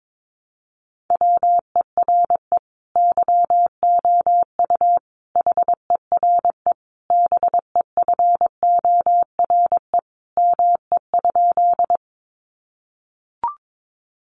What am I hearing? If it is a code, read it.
Morse code, heard as a keyed tone, about 22 wpm, 700 Hz. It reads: WERE YOU HERE BEFORE ME?
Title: Sentences from Top 200 Words 22wpm